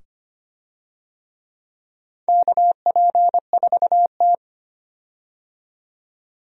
Morse code, heard as a keyed tone, about 25 wpm, 700 Hz. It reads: KP4T
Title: DXCC-100 Call Signs - 4 Characters - 25wpm